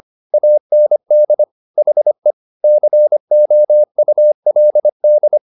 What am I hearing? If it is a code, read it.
Morse code, heard as a keyed tone, about 25 wpm, 600 Hz. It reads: AND HE COULD
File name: SS305 25